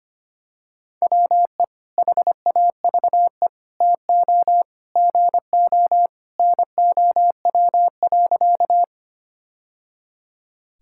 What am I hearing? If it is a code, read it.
Morse code, heard as a keyed tone, about 25 wpm, 700 Hz. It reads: WE HAVE TO GO NOW.
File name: practice / 25wpm / Sentences with 3-Word Phrases - 25wpm